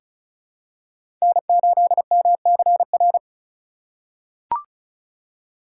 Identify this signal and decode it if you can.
Morse code, heard as a keyed tone, about 34 wpm, 700 Hz. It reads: N8MCR
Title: USA Call Signs - 5 Characters - 4x Character Spacing - 35wpm